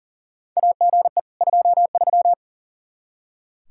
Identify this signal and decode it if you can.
Morse code, heard as a keyed tone, about 40 wpm, 700 Hz. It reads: AGE 23
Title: K1USN SST QSO Practice 40wpm